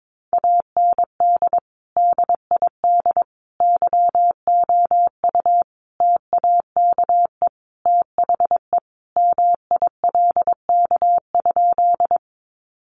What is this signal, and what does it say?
Morse code, heard as a keyed tone, about 22 wpm, 700 Hz. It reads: AND DID YOU TAKE THE MILK?
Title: Sentences from Top 1500 Words 22wpm